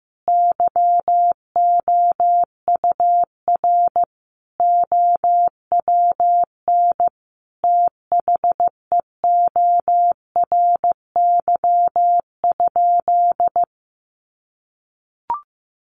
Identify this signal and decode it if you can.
Morse code, heard as a keyed tone, about 15 wpm, 700 Hz. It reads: YOUR OWN THEORY?
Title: Sentences from Top 1000 Words 15wpm